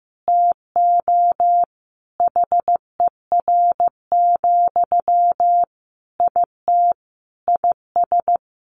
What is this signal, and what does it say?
Morse code, heard as a keyed tone, about 15 wpm, 700 Hz. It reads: TO HER, IT IS
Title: Sentences from Top 800 Words 15wpm